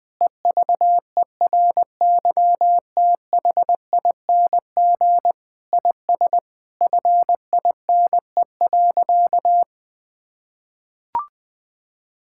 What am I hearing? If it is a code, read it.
Morse code, heard as a keyed tone, about 19 wpm, 700 Hz. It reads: EVERYTHING IS FINE.
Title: Sentences with 6-Word Phrases - 20wpm